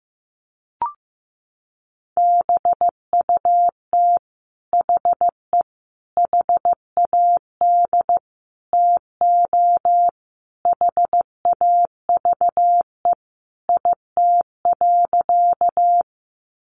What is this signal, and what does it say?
Morse code, heard as a keyed tone, about 15 wpm, 700 Hz. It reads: BUT HE HAD TO HAVE IT.
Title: Sentences from Top 100 Words 15wpm